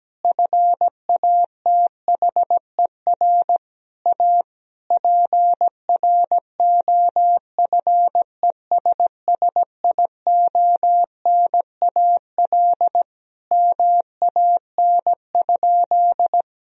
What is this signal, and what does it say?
Morse code, heard as a keyed tone, about 17 wpm, 700 Hz. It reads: FATHER A PROFESSIONAL MAN?
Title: Sentences from Top 1000 Words - Review Part 4 - 17wpm